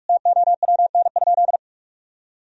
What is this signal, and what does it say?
Morse code, heard as a keyed tone, about 41 wpm, 700 Hz. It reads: TOWN?
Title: Sentences from Top 1000 Words - Review Part 3 - 45wpm